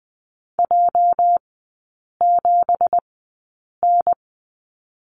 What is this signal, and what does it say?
Morse code, heard as a keyed tone, about 19 wpm, spaced as for 8 wpm, 690 Hz. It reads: J7N
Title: 212 Lesson 34e - Number 8 - Three Characters 20wpm